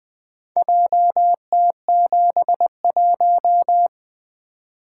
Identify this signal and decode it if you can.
Morse code, heard as a keyed tone, about 20 wpm, 700 Hz. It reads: JT71
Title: Four Letter-Number Combinations - 20wpm